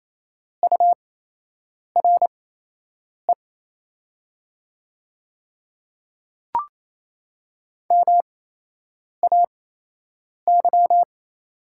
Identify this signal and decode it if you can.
Morse code, heard as a keyed tone, about 28 wpm, spaced as for 8 wpm, 700 Hz. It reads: URE MAY
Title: 8x Character Spacing - Top 300 Words - 28wpm